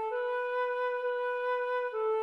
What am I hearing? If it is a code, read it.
There is no code.